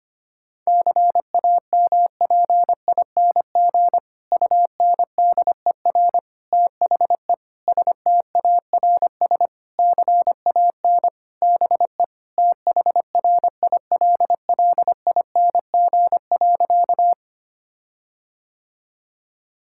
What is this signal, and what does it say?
Morse code, heard as a keyed tone, about 25 wpm, 700 Hz. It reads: CAMPING UNDER THE STARS CAN BE THRILLING.
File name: practice / 25wpm / Difficult Sentences with Hints - 25wpm